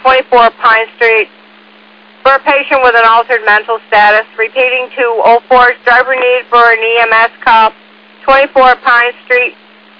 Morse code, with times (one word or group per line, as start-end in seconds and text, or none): none